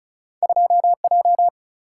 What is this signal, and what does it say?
Morse code, heard as a keyed tone, about 36 wpm, 700 Hz. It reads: 2J
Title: Single-Letter Single-Number 35wpm